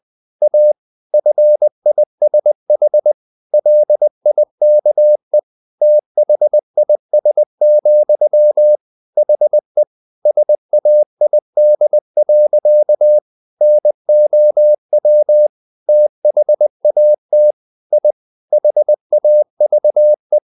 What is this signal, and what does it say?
Morse code, heard as a keyed tone, about 20 wpm, 600 Hz. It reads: A FISH LIKE THIS, HE SAID. NOW THAT I HAVE
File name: the old man and the sea part 2 20wpm